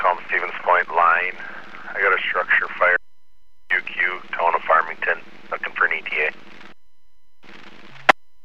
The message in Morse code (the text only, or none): none